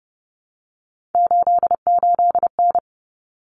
Morse code, heard as a keyed tone, about 30 wpm, 700 Hz. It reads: 88N
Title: Common Cut Numbers - 3-Digit 30wpm